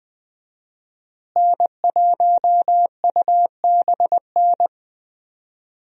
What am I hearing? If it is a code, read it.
Morse code, heard as a keyed tone, about 20 wpm, 700 Hz. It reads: N1UBN
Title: USA Call Signs - 5 Characters - Repeated 2x - 20wpm